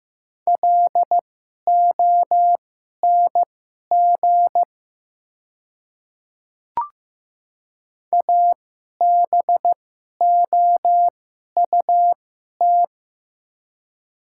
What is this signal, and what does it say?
Morse code, heard as a keyed tone, about 15 wpm, spaced as for 11 wpm, 700 Hz. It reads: LONG ABOUT
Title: 2x Character Spacing - Top 100 Words - 15wpm